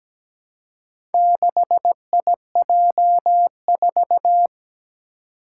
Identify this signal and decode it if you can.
Morse code, heard as a keyed tone, about 17 wpm, 700 Hz. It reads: 6IJ4
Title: Four Challenging Letter-Number Combinations 17wpm